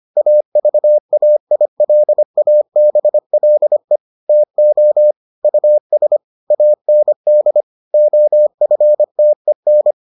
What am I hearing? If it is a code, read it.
Morse code, heard as a keyed tone, about 25 wpm, 600 Hz. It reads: AVAILABLE TO US AND OFTEN